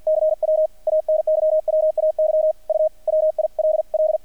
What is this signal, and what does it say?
Morse code, heard as a keyed tone, about 50 wpm, 620 Hz. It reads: CW ANYWAY U WILL
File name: archived / CRSnet